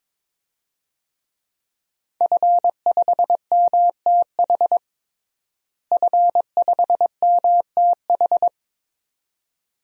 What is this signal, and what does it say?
Morse code, heard as a keyed tone, about 22 wpm, 700 Hz. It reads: F5MTH F5MTH
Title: DXCC-100 Call Signs - 5 Characters - Repeated 2x - 22wpm